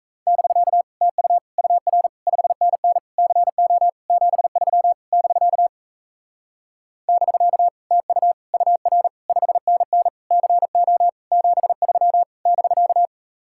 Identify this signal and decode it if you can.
Morse code, heard as a keyed tone, about 37 wpm, 700 Hz. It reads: <BK> TU UR 5NN CO 73 <BK> <BK> TU UR 5NN CO 73 <BK>